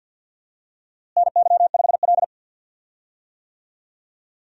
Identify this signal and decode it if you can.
Morse code, heard as a keyed tone, about 51 wpm, 700 Hz. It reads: NY5L
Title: USA Call Signs - 4 Characters - Repeated 2x - 50wpm